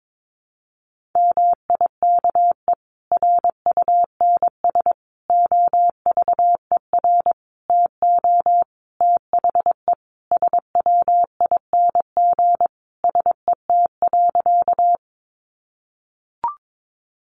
Morse code, heard as a keyed tone, about 22 wpm, 700 Hz. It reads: MIKE RUNS OVER TO THE SWING SET.
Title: Additional Easy Phrases 22wpm